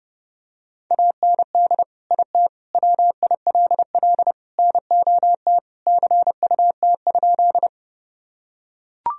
Morse code, heard as a keyed tone, about 32 wpm, 700 Hz. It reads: AND IT WILL NOT CUT?
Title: Sentences from Top 1000 Words - Review Part 1 - 30wpm